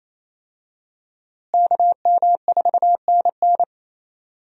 Morse code, heard as a keyed tone, about 28 wpm, 700 Hz. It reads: KM4NN
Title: USA Call Signs - 5 Characters - Repeated 2x - 28wpm